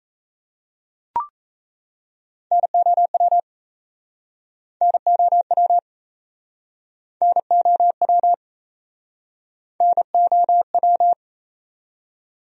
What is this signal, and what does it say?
Morse code, heard as a keyed tone, about 39 wpm, 700 Hz. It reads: NOW NOW NOW NOW